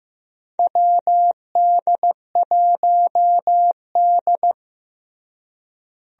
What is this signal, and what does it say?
Morse code, heard as a keyed tone, about 16 wpm, 700 Hz. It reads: WD1D
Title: USA Call Signs - 4 Characters - Repeated 2x - 15wpm